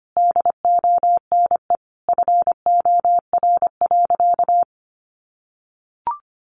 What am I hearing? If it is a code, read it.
Morse code, heard as a keyed tone, about 25 wpm, 700 Hz. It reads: DONE FOR.
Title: Sentences from Top 1500 Words 25wpm